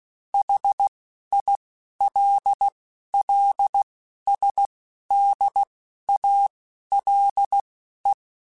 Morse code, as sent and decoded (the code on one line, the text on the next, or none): .... .. .-.. .-.. ... -.. .- .-.. .
HILLSDALE